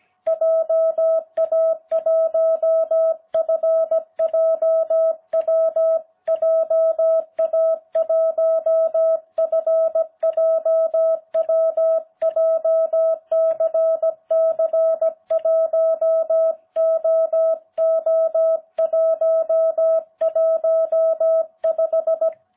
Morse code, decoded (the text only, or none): JA1FJW JA1FJW JCC1OO115